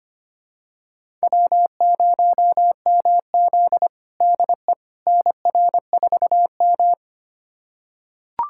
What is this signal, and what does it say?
Morse code, heard as a keyed tone, about 25 wpm, 700 Hz. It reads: W0MZ DE NR4M
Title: USA Callsign DE Callsign - 4 Characters - 25wpm